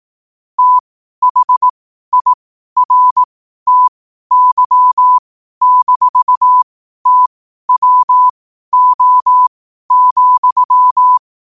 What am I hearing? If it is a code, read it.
Morse code, heard as a keyed tone, about 18 wpm, spaced as for 12 wpm, 1000 Hz. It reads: THIRTY-TWO,
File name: qotdfiles / 12wpm